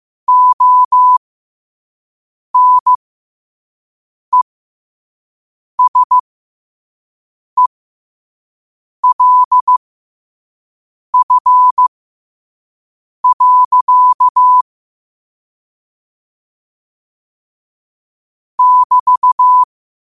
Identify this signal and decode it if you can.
Morse code, heard as a keyed tone, about 15 wpm, spaced as for 5 wpm, 1000 Hz. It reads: ONESELF. <BT>
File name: qotdfiles / 5wpm